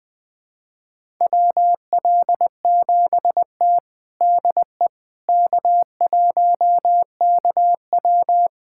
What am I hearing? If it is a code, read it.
Morse code, heard as a keyed tone, about 21 wpm, 700 Hz. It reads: WL7T DE K1KW
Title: USA Callsign DE Callsign - 4 Characters - 20wpm